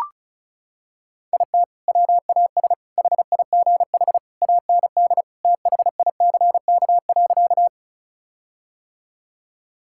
Morse code, heard as a keyed tone, about 36 wpm, 700 Hz. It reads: IT WAS HIGH AND THICK.
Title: Sentences from Top 1600 Words 35wpm